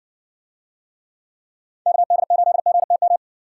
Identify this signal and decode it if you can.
Morse code, heard as a keyed tone, about 60 wpm, 700 Hz. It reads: KD8GTM